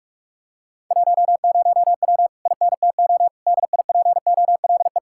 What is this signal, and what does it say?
Morse code, heard as a keyed tone, about 40 wpm, 700 Hz. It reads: 10W INTO DIPOLE